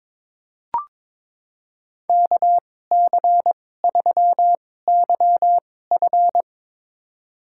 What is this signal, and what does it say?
Morse code, heard as a keyed tone, about 22 wpm, spaced as for 16 wpm, 700 Hz. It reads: KC3YF
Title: USA Call Signs - 5 Characters - 2x Character Spacing - 22wpm